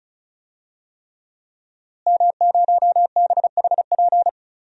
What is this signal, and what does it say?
Morse code, heard as a keyed tone, about 36 wpm, 700 Hz. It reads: M0BHP
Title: DXCC-100 Call Signs - 5 Characters - Repeated 3x - 35wpm